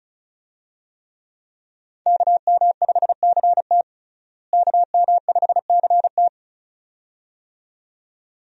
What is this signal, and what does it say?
Morse code, heard as a keyed tone, about 35 wpm, 700 Hz. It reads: KM5CT KM5CT
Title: USA Call Signs - 5 Characters - Repeated 2x - 35wpm